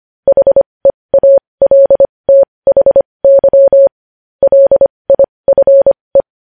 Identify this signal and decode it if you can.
Morse code, heard as a keyed tone, about 25 wpm, spaced as for 20 wpm, 550 Hz. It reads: HEALTHY LIFE
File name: podcast / News headlines-1x-25x20